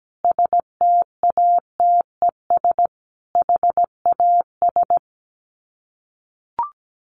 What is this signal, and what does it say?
Morse code, heard as a keyed tone, about 17 wpm, 700 Hz. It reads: STATES HAS
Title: Sets of 5 Words Encore 17wpm